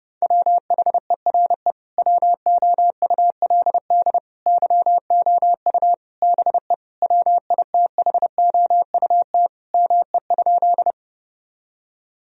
Morse code, heard as a keyed tone, about 30 wpm, 700 Hz. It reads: WHERE WOULD YOU BE WITHOUT ME?
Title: Sentences from Top 1000 Words - Review Part 3 - 30wpm